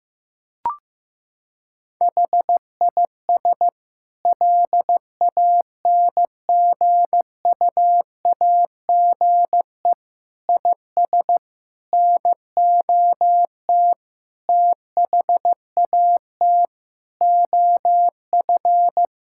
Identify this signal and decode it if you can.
Morse code, heard as a keyed tone, about 15 wpm, 700 Hz. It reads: HIS LANGUAGE IS NOT THAT OF
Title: Sentences from Top 1600 Words 15wpm